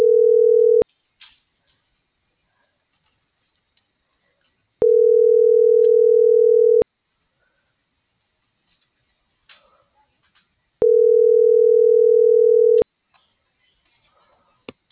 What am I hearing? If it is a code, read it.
Ringback tone (North America), 440 + 480 Hz.